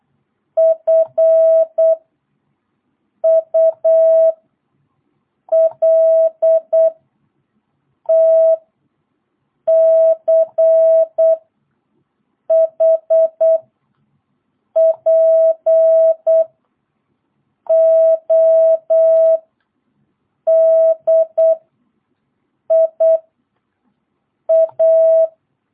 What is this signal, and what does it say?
Morse code, heard as a keyed tone, about 8 wpm, 660 Hz. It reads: F U L T C H P O D I A
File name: jh8cbh / 2022cw